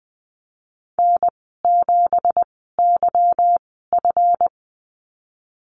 Morse code, heard as a keyed tone, about 20 wpm, spaced as for 15 wpm, 700 Hz. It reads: N7YF
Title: 173 Lesson 28g - Number 9 - Call Signs 20wpm